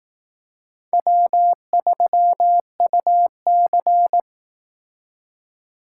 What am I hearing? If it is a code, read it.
Morse code, heard as a keyed tone, about 18 wpm, 700 Hz. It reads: W3UC
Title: Speed Racing - USA Call Signs 15wpm